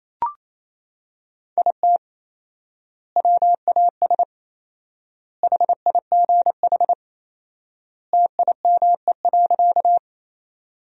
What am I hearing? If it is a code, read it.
Morse code, heard as a keyed tone, about 28 wpm, 700 Hz. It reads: IT WAS HIGH TIME.
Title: 4x Word-Spacing - Sentences from Top 200 Words - 28wpm